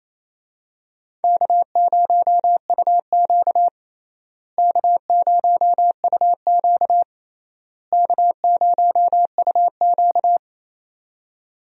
Morse code, heard as a keyed tone, about 28 wpm, 700 Hz. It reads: K0UQ K0UQ K0UQ
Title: USA Call Signs - 4 Characters - Repeated 3x - 28wpm